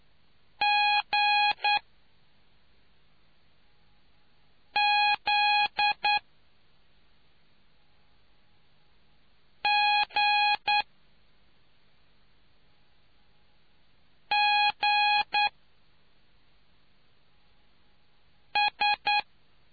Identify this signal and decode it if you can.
Morse code, heard as a keyed tone, about 9 wpm, spaced as for 2 wpm, 830 Hz. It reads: GZGGS